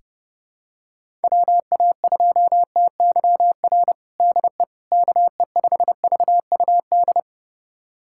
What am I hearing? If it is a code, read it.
Morse code, heard as a keyed tone, about 30 wpm, 700 Hz. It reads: WA2TYR DE KE5VUD
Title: USA Callsign DE Callsign - 6 Characters - 30wpm